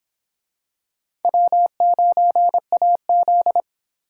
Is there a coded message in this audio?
Yes: Morse code from 1.2 to 3.6 s, heard as a keyed tone, about 26 wpm, 700 Hz. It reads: W9AZ